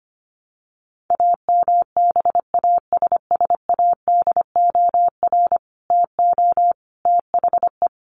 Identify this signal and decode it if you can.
Morse code, heard as a keyed tone, about 25 wpm, 700 Hz. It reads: AMBASSADOR TO THE
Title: Sets of 5 Words Encore2 25wpm